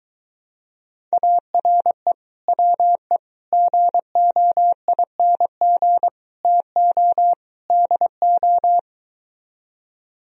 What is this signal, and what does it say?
Morse code, heard as a keyed tone, about 23 wpm, 690 Hz. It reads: ARE WE GOING TO DO